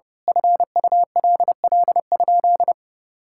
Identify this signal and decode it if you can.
Morse code, heard as a keyed tone, about 30 wpm, 700 Hz. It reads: FULL?
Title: Sentences from Top 1600 Words 30wpm